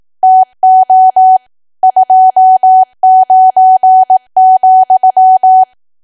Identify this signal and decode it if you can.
Morse code, heard as a keyed tone, about 18 wpm, 750 Hz. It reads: TO 29,
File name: cwa / W1AW-Level2 / 18WPM